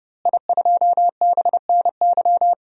Morse code, heard as a keyed tone, about 30 wpm, 700 Hz. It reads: I2BNY
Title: DXCC-100 Call Signs - 5 Characters - Repeated 3x - 30wpm